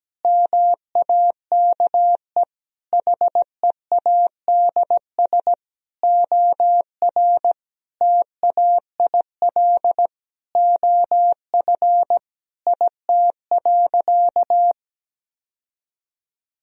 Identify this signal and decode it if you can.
Morse code, heard as a keyed tone, about 17 wpm, 700 Hz. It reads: MAKE HEADS OR TAIL OF IT.
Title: Sentences from Top 1000 Words - Review Part 3 - 17wpm